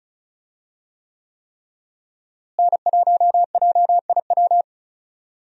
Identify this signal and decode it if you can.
Morse code, heard as a keyed tone, about 34 wpm, 700 Hz. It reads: N1JIW
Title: USA Call Signs - 5 Characters - Repeated 3x - 35wpm